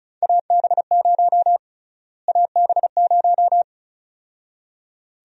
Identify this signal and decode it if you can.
Morse code, heard as a keyed tone, about 36 wpm, 700 Hz. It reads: AB0 AB0
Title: USA Call Sign Prefixes - Repeated 2x - 35wpm